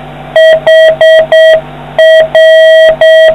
Morse code, heard as a keyed tone, about 6 wpm, 1950 Hz, with noise. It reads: HR